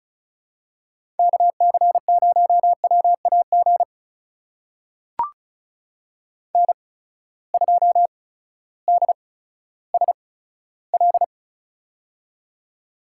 Morse code, heard as a keyed tone, about 36 wpm, spaced as for 16 wpm, 700 Hz. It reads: KC0WAG N2DSL
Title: USA Call Signs - 8x Character Spacing - 35wpm